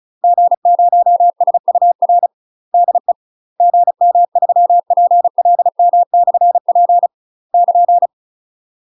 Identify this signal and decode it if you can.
Morse code, heard as a keyed tone, about 33 wpm, 700 Hz. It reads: G0SUR DE GM3PLM/P <KN>
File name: UK MASTER QSOs TRAINING / 35WPM / UK MASTER QSOs 0029